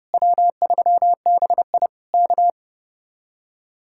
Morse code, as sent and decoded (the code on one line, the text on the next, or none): .-- ...-- -... .. / -.-
W3BI K